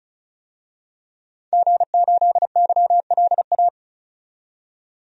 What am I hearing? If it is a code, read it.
Morse code, heard as a keyed tone, about 34 wpm, 700 Hz. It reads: G8YLA
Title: DXCC-100 Call Signs - 5 Characters - 35wpm